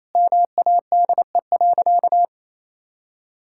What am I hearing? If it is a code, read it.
Morse code, heard as a keyed tone, about 28 wpm, 700 Hz. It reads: MADE.
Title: Sentences from Top 1000 Words - Review Part 5 - 28wpm